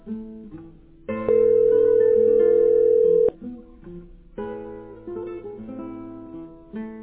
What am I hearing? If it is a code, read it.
Ringback tone (North America), 440 + 480 Hz.